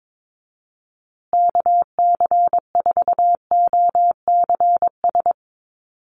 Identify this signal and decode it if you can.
Morse code, heard as a keyed tone, about 22 wpm, 700 Hz. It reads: KC4OCS
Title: USA Call Signs Repeated 2x 22wpm